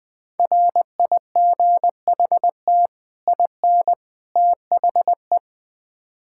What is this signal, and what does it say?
Morse code, heard as a keyed tone, about 21 wpm, 700 Hz. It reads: RIGHT IN THE
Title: Build A Phrase - 5-Word Phrases - 20wpm